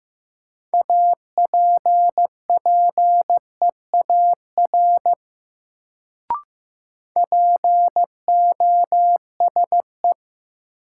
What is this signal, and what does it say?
Morse code, heard as a keyed tone, about 15 wpm, 700 Hz. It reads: APPEAR POSE
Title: Sentences from Top 1000 Words - Review Part 2 - New Word Drill - 15wpm